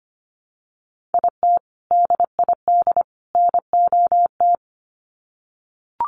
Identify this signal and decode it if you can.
Morse code, heard as a keyed tone, about 25 wpm, 700 Hz. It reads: IT DID NOT
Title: Sets of 3 Words 25wpm